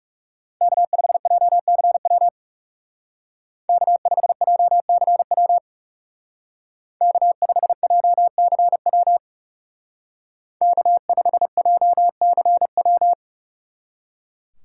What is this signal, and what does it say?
Morse code, heard as a keyed tone, about 37 wpm, 700 Hz. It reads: K5JCW K5JCW K5JCW K5JCW
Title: Speed Racing - USA 1x3 Call Signs 30wpm